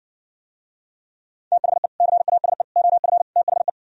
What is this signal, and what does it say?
Morse code, heard as a keyed tone, about 59 wpm, 700 Hz. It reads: THE CASE OF THE